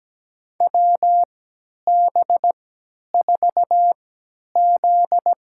Morse code, heard as a keyed tone, about 17 wpm, spaced as for 10 wpm, 700 Hz. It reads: WB4Z
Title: USA Call Signs - 3x Character Spacing - 17wpm